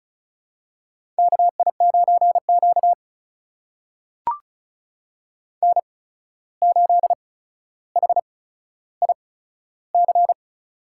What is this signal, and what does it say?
Morse code, heard as a keyed tone, about 36 wpm, spaced as for 14 wpm, 700 Hz. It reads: KI9Q N8HIC
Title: USA Call Signs - 8x Character Spacing - 35wpm